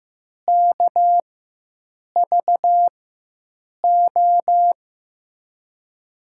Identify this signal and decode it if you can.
Morse code, heard as a keyed tone, about 15 wpm, spaced as for 7 wpm, 700 Hz. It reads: KVO
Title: USA Call Signs - 6 Characters - 4x Character Spacing - 15wpm